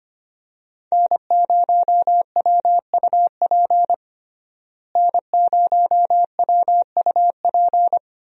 Morse code, heard as a keyed tone, about 25 wpm, 700 Hz. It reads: N0WUP N0WUP
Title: USA Call Signs - 5 Characters - Repeated 2x - 25wpm